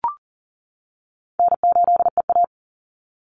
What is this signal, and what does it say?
Morse code, heard as a keyed tone, about 40 wpm, 700 Hz. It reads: N8EA
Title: Call Signs - 40wpm